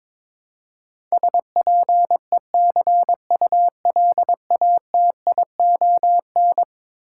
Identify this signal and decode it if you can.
Morse code, heard as a keyed tone, about 22 wpm, 700 Hz. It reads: SPECULATION
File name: practice / 22wpm / Eleven Letter Words - 22wpm